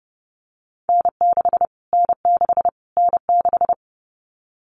Morse code, heard as a keyed tone, about 30 wpm, 700 Hz. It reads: N6 N6 N6
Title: Callsign Prefixes for Top 25 DXCC - Repeated 3x - 30wpm